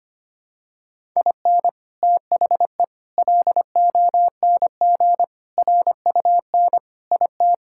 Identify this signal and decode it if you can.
Morse code, heard as a keyed tone, about 25 wpm, 700 Hz. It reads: IN THE LONG RUN IT
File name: practice / 25wpm / Sentences with 4-Word Phrases - 25wpm